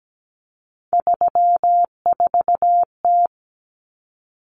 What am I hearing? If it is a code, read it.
Morse code, heard as a keyed tone, about 17 wpm, 700 Hz. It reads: 34T
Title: Common Cut Numbers - 3-Digit 17wpm